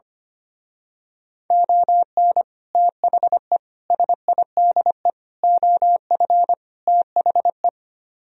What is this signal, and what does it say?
Morse code, heard as a keyed tone, about 25 wpm, 700 Hz. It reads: ON THE SIDE OF THE
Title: Speed-Racing - Sets of 5 Words 25wpm